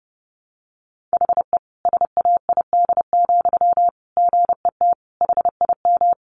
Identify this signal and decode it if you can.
Morse code, heard as a keyed tone, about 30 wpm, 700 Hz. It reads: HE SAID, GET HIM
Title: Sentences from Top 400 Words 30wpm